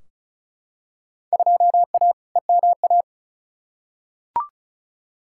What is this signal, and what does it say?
Morse code, heard as a keyed tone, about 34 wpm, 700 Hz. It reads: 2A EMA
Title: Field Day Exchanges - 35wpm